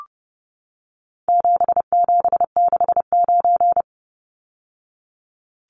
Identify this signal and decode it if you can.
Morse code, heard as a keyed tone, about 30 wpm, 700 Hz. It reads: 7769
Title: Numbers - 4-Digit 30wpm